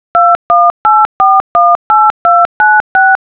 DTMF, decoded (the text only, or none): 218418296